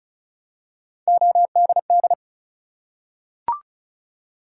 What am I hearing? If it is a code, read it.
Morse code, heard as a keyed tone, about 36 wpm, 700 Hz. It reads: ODD
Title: Three Letter Words - 35wpm